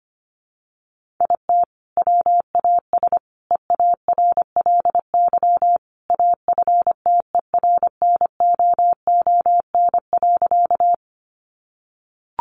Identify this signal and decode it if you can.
Morse code, heard as a keyed tone, about 25 wpm, 700 Hz. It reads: IT WAS EARLY AFTERNOON.
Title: Sentences from Top 1500 Words 25wpm